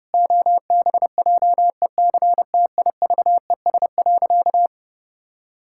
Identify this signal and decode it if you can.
Morse code, heard as a keyed tone, about 30 wpm, 700 Hz. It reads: OBJECTIVES.
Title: Pangram Challenge v2 - 30wpm